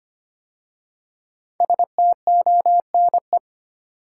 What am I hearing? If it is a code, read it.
Morse code, heard as a keyed tone, about 25 wpm, 700 Hz. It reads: STONE
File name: 5 Letter Words - First 3 Letters Given - 25wpm